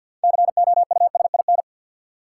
Morse code, heard as a keyed tone, about 53 wpm, 700 Hz. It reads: COUSIN
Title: Six Letter Words - 50wpm